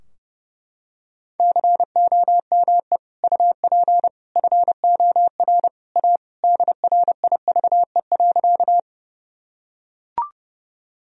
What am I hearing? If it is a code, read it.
Morse code, heard as a keyed tone, about 32 wpm, 700 Hz. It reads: COME UP FOR A DRIVE.
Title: Sentences from Top 1000 Words - Review Part 2 - 30wpm